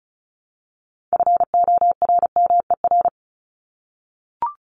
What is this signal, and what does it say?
Morse code, heard as a keyed tone, about 34 wpm, 700 Hz. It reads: FORMER
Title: LY Suffix - Top 100 Words with Base Words 35wpm